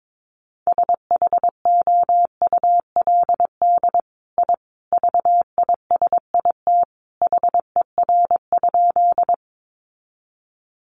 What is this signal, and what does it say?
Morse code, heard as a keyed tone, about 22 wpm, 700 Hz. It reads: SHOULD I VISIT HER?
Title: Sentences from Top 900 Words 22wpm